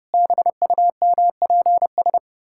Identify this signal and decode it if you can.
Morse code, heard as a keyed tone, about 32 wpm, 700 Hz. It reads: BUMPS
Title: Pangram Challenge v2 - 30wpm